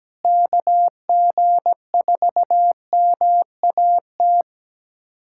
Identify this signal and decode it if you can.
Morse code, heard as a keyed tone, about 17 wpm, 700 Hz. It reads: KG4MAT
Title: USA Callsign DE Callsign 17wpm